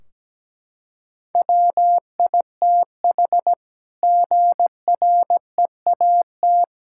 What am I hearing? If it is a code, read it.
Morse code, heard as a keyed tone, about 17 wpm, 700 Hz. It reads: WITH GREAT
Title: Sentences from Top 1600 Words 17wpm